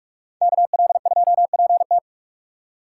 Morse code, heard as a keyed tone, about 46 wpm, 700 Hz. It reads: KL2PT
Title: USA Call Signs - 5 Characters - 3x Character Spacing - 45wpm